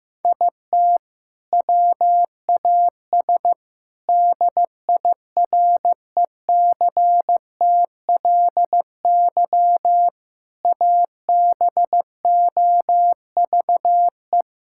Morse code, heard as a keyed tone, about 15 wpm, 700 Hz. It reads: IT WAS DIRECTLY ABOVE
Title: Sentences from Top 1000 Words - Review Part 5 - 15wpm